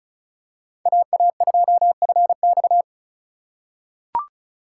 Morse code, heard as a keyed tone, about 34 wpm, 700 Hz. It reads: AA2FX
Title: USA Call Signs - 5 Characters - 3x Character Spacing - 35wpm